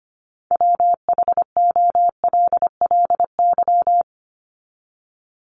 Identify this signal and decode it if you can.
Morse code, heard as a keyed tone, about 25 wpm, 700 Hz. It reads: WHOLLY
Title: LY Suffix - Top 100 Words with Base Words 25wpm